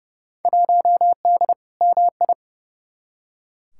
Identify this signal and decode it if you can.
Morse code, heard as a keyed tone, about 31 wpm, 700 Hz. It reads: 1D MI
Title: Field Day Exchanges - 30wpm